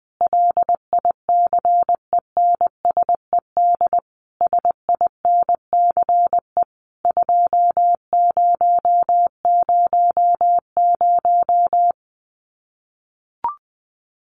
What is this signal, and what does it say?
Morse code, heard as a keyed tone, about 20 wpm, 700 Hz. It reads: LICENSED SINCE 2000
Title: Common QSO Phrases 20wpm